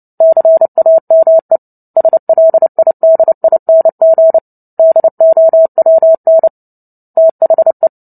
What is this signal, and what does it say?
Morse code, heard as a keyed tone, about 30 wpm, 650 Hz. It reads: CAME SLIDING DOWN THE
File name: Book Two / 03-Chapter 3 - 29 WPM